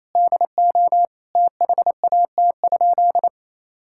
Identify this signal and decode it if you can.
Morse code, heard as a keyed tone, about 27 wpm, 700 Hz. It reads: DO THAT?
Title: Sentences from Top 1000 Words - Review Part 6 - 28wpm